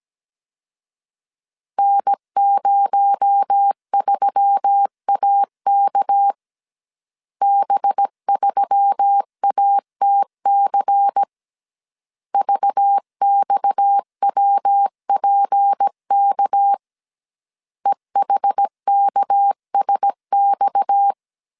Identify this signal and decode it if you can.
Morse code, heard as a keyed tone, about 17 wpm, 800 Hz. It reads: N03AK B3ATC VXWPK EHKSX
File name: 16wpm